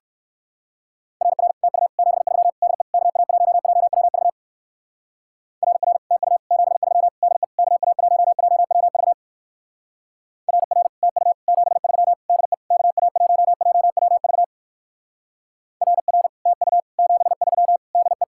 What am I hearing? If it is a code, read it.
Morse code, heard as a keyed tone, about 59 wpm, 700 Hz. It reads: RR TU 73 DE KA1JWV RR TU 73 DE KA1JWV RR TU 73 DE KA1JWV RR TU 73 DE